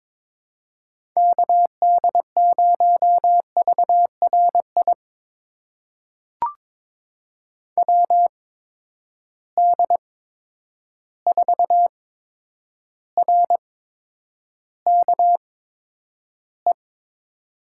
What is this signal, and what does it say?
Morse code, heard as a keyed tone, about 22 wpm, spaced as for 9 wpm, 700 Hz. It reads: KD0VRI WD4RKE
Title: USA Call Signs - 6 Characters - 8x Character Spacing - 22wpm